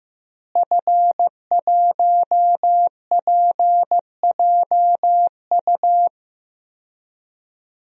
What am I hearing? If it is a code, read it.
Morse code, heard as a keyed tone, about 15 wpm, 700 Hz. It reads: F1PJU
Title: DXCC-100 Call Signs - 5 Characters - Repeated 2x - 15wpm